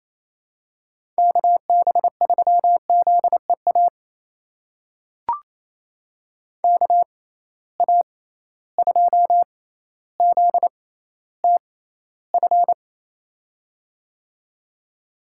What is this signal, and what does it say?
Morse code, heard as a keyed tone, about 28 wpm, spaced as for 14 wpm, 700 Hz. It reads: KB3ZEA KA2ZTF
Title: USA Call Signs - 6 Characters - 6x Character Spacing - 28wpm